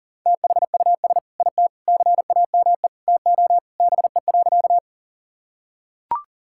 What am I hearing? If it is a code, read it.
Morse code, heard as a keyed tone, about 40 wpm, 700 Hz. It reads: THUS IT CAME TO BE.
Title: Sentences from Top 1000 Words - Review Part 5 - 40wpm